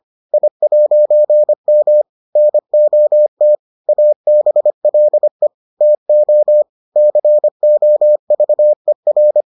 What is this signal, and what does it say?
Morse code, heard as a keyed tone, about 25 wpm, 600 Hz. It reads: I'M NOT ABLE TO COVER